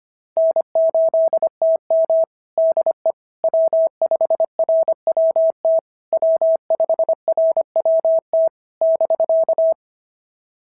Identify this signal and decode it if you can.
Morse code, heard as a keyed tone, about 22 wpm, 650 Hz. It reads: N8TM DE W5RWT W5RWT <BK>
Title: SOTA QSO Practice 25wpm